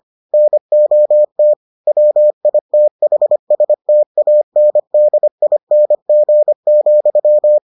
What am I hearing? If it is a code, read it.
Morse code, heard as a keyed tone, about 25 wpm, 600 Hz. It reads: NOT WITHSTANDING,